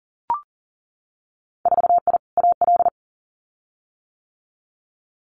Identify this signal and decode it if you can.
Morse code, heard as a keyed tone, about 40 wpm, 700 Hz. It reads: 4I AL
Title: Winter Field Day Exchanges Encore 40wpm